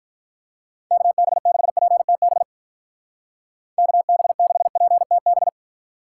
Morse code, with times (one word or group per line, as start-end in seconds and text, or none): 0.9-2.4 KB6PTB
3.8-5.5 KB6PTB